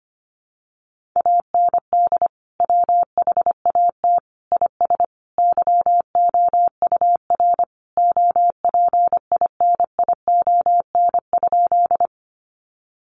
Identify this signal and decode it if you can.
Morse code, heard as a keyed tone, about 25 wpm, 700 Hz. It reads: AND WHAT IS YOUR OPINION?
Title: Sentences from Top 1200 Words 25wpm